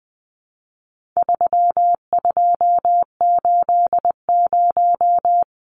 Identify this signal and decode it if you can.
Morse code, heard as a keyed tone, about 20 wpm, 700 Hz. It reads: 3280
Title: Numbers - 4-Digit 20wpm